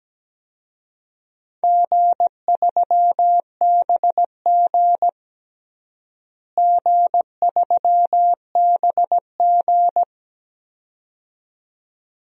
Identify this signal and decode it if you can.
Morse code, heard as a keyed tone, about 17 wpm, 700 Hz. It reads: G3BG G3BG
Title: DXCC-100 Call Signs - 4 Characters - Repeated 2x - 17wpm